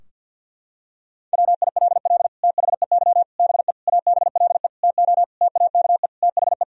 Morse code, heard as a keyed tone, about 50 wpm, 700 Hz. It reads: WILL THEY BE ABLE TO TAKE THE
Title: Sentences from Top 1000 Words - Review Part 3 - 50wpm